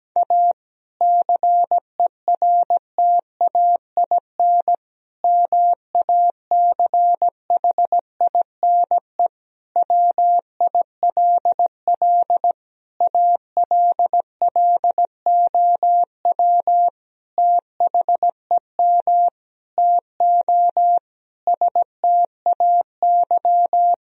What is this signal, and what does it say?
Morse code, heard as a keyed tone, about 17 wpm, 700 Hz. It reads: A CERTAIN MACHINE WILL ALLOW THEM TO STAY
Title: Sentences from Top 1000 Words - Review Part 2 - 17wpm